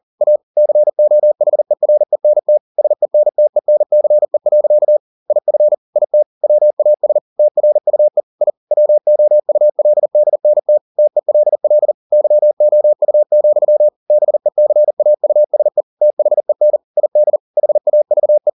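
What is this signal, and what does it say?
Morse code, heard as a keyed tone, about 40 wpm, 600 Hz. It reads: A COHERENT SENTENCE. IF IT WAS TRUE I WOULDNT TELL YOU, BECAUSE THEN ID HAVE
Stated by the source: Morse code at 40 wpm, practice text, CW med Gnister.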